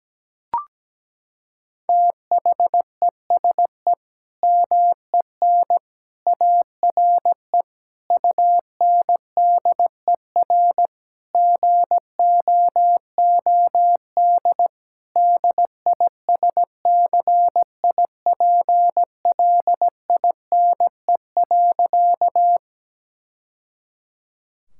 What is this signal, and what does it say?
Morse code, heard as a keyed tone, about 17 wpm, 700 Hz. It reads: THESE MEN ARE UNDER GOOD DISCIPLINE.
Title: Sentences from Top 1000 Words - Review Part 6 - 17wpm